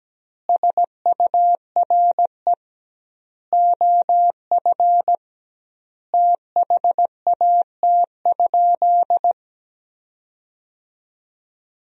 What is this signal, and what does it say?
Morse code, heard as a keyed tone, about 17 wpm, 700 Hz. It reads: SURE OF THAT?
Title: 2x Word-Spacing - Sentences from Top 300 Words - 17wpm